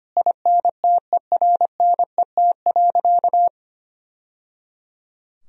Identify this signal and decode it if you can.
Morse code, heard as a keyed tone, about 25 wpm, 700 Hz. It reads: INTERNET.
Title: Sentences from Top 1000 Words - Review Part 4 - 25wpm